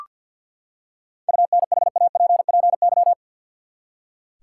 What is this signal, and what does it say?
Morse code, heard as a keyed tone, about 46 wpm, 700 Hz. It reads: UNHAPPY
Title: Seven Letter Words - 50wpm